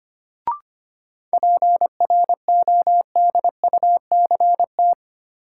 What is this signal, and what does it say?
Morse code, heard as a keyed tone, about 25 wpm, 700 Hz. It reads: PRODUCT
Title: Sentences from Top 1400 Words - New Word Drill Alt - 25wpm